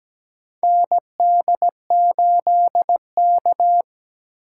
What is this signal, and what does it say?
Morse code, heard as a keyed tone, about 17 wpm, 700 Hz. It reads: ND8K
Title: USA Call Signs - 4 Characters - 4x Character Spacing - 17wpm